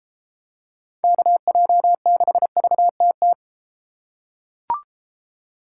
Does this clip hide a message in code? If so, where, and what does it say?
Yes: Morse code from 1.0 to 3.3 s, heard as a keyed tone, about 32 wpm, 700 Hz. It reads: KJ6VTT